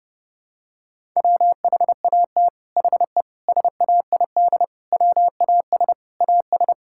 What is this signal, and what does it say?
Morse code, heard as a keyed tone, about 32 wpm, 700 Hz. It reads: WHAT HE SAID WAS AS
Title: Sentences from Top 1000 Words - Review Part 6 - 30wpm